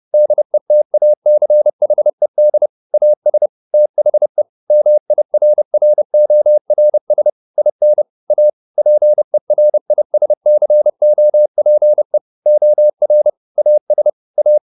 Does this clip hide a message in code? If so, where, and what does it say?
Yes: Morse code from 0.1 to 14.6 s, heard as a keyed tone, about 30 wpm, 600 Hz. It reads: DETACHED AS THE MIRRORS IN A PERISCOPE OR AS A